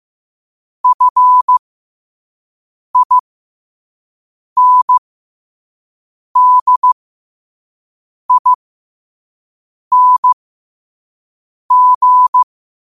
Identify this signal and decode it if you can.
Morse code, heard as a keyed tone, about 15 wpm, spaced as for 5 wpm, 1000 Hz. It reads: FINDING
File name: qotdfiles / 5wpm